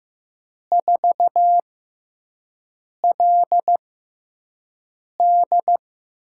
Morse code, heard as a keyed tone, about 15 wpm, spaced as for 5 wpm, 700 Hz. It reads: 4LD
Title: USA Call Signs - 4 Characters - 6x Character Spacing - 15wpm